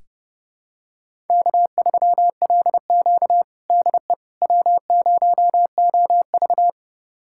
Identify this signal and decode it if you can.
Morse code, heard as a keyed tone, about 30 wpm, 700 Hz. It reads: K3LQ DE W0OV
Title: USA Callsign DE Callsign - 4 Characters - 30wpm